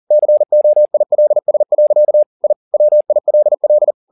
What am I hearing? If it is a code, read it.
Morse code, heard as a keyed tone, about 40 wpm, 600 Hz. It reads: COILS. I WILL